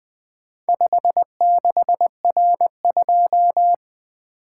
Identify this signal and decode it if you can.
Morse code, heard as a keyed tone, about 20 wpm, 700 Hz. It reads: 56R2
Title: Four Letter-Number Combinations - 20wpm